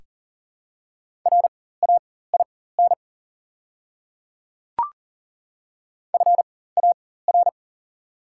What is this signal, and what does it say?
Morse code, heard as a keyed tone, about 40 wpm, spaced as for 19 wpm, 690 Hz. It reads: RAIN FAR
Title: 4x Character Spacing - Top 300 Words - 40wpm